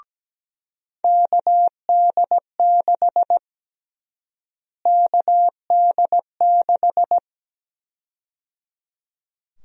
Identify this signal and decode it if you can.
Morse code, heard as a keyed tone, about 17 wpm, 700 Hz. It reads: KD6 KD6
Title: USA Call Sign Prefixes - Repeated 2x - 17wpm